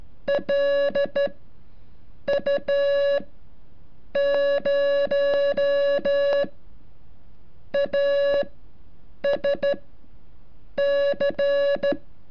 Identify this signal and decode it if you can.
Morse code, heard as a keyed tone, about 12 wpm, spaced as for 6 wpm, 560 Hz. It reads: LU0ASC
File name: curso / ejercicio 33